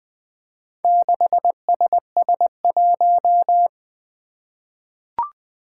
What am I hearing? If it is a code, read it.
Morse code, heard as a keyed tone, about 20 wpm, 700 Hz. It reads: 6SS1
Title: Four Challenging Letter-Number Combinations 20wpm